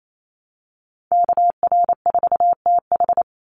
Morse code, heard as a keyed tone, about 28 wpm, 690 Hz. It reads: KR4TH
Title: Call Signs Repeated 3x - 28wpm